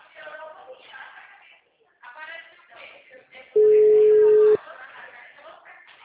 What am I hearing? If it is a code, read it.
Ringback tone (Europe, 425 Hz), with speech.
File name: out- -2006- - - .63823